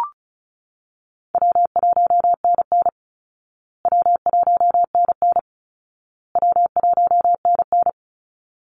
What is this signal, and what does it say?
Morse code, heard as a keyed tone, about 34 wpm, 700 Hz. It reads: W1NN W1NN W1NN
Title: Easy Call Signs Repeated 3x - 35wpm